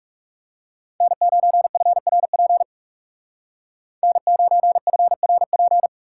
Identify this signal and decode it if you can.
Morse code, heard as a keyed tone, about 37 wpm, 700 Hz. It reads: N9FRP N9FRP
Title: Speed Racing - USA 1x3 Call Signs 30wpm